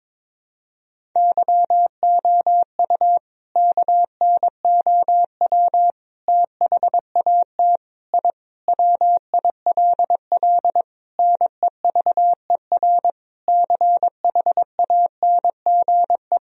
Morse code, heard as a keyed tone, about 22 wpm, 700 Hz. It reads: YOU KNOW THAT I WILL NEVER CHANGE
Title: Sentences from Top 1000 Words - Review Part 1 - 22wpm